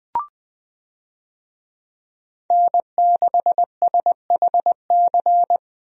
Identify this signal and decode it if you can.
Morse code, heard as a keyed tone, about 21 wpm, 700 Hz. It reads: N6SHC
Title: USA Call Signs - 5 Characters - Repeated 3x - 20wpm